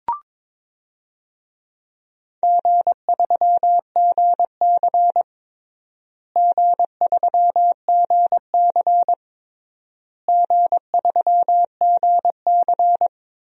Morse code, heard as a keyed tone, about 22 wpm, 700 Hz. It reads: G3GC G3GC G3GC
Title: DXCC-100 Call Signs - 4 Characters - Repeated 3x - 22wpm